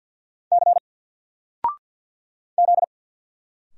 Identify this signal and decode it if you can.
Morse code, heard as a keyed tone, about 48 wpm, 700 Hz. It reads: C Z
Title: Single Character (Letter:Number:Punctuation:Prosign) 50wpm